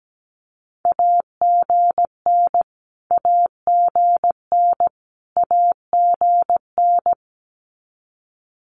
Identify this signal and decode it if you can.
Morse code, heard as a keyed tone, about 17 wpm, 700 Hz. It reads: AGN AGN AGN
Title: Abbreviations Repeated 3x - 17wpm